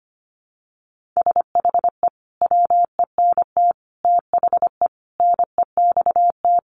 Morse code, heard as a keyed tone, about 25 wpm, 700 Hz. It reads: SHE WENT THE NEXT
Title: Sentences from Top 1500 Words 25wpm